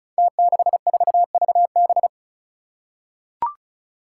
Morse code, heard as a keyed tone, about 36 wpm, 700 Hz. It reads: T64VB
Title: Five Challenging Letter-Number Combinations 35wpm